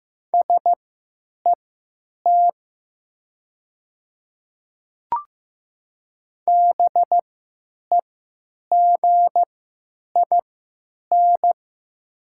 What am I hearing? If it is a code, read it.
Morse code, heard as a keyed tone, about 15 wpm, spaced as for 9 wpm, 700 Hz. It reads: SET BEGIN
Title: 3x Character Spacing - Top 300 Words - 15wpm